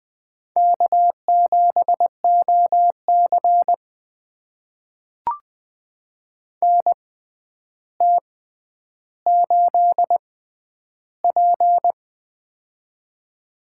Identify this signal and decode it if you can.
Morse code, heard as a keyed tone, about 21 wpm, spaced as for 10 wpm, 700 Hz. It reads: K7OC NT8P
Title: USA Call Signs - 4 Characters - 6x Character Spacing - 20wpm